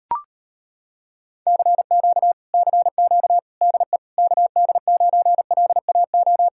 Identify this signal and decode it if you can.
Morse code, heard as a keyed tone, about 33 wpm, 700 Hz. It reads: CQ CQ DE KD9LAO